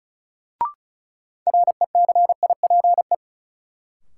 Morse code, heard as a keyed tone, about 34 wpm, 700 Hz. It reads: RECIPE
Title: Sentences from Top 1200 Words - New Word Drill - 35wpm